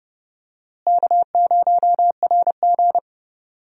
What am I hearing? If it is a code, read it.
Morse code, heard as a keyed tone, about 32 wpm, 700 Hz. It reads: K0RG
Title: USA Call Signs - 4 Characters - Repeated 2x - 30wpm